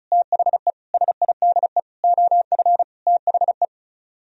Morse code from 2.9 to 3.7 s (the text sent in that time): THE